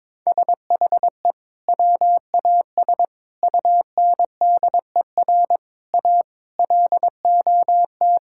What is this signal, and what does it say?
Morse code, heard as a keyed tone, about 22 wpm, 700 Hz. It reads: SHE WAS UNDER A LOT
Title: Sentences from Top 1000 Words - Review Part 6 - 22wpm